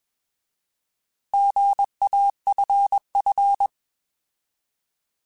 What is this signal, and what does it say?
Morse code, heard as a keyed tone, about 21 wpm, 800 Hz. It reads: GAFF
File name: Alphabetical / Alphabetical mp3s / Gs @20